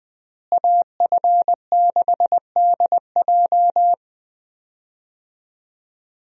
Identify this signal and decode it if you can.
Morse code, heard as a keyed tone, about 20 wpm, 700 Hz. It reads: AF6DJ
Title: USA Call Signs - 5 Characters - Repeated 3x - 20wpm